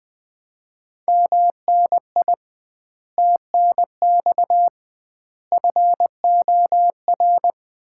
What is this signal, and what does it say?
Morse code, heard as a keyed tone, about 20 wpm, 700 Hz. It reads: MNI TNX FOR
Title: Common QSO Phrases - 2x Word-Spacing - 20wpm